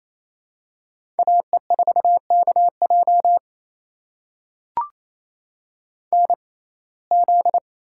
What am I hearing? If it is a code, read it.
Morse code, heard as a keyed tone, about 28 wpm, spaced as for 20 wpm, 700 Hz. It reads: AE4KJ NZ
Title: USA Call Signs - 6x Character Spacing - 28wpm